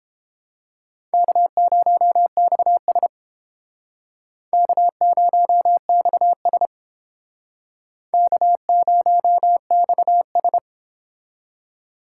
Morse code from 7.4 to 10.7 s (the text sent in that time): K0XS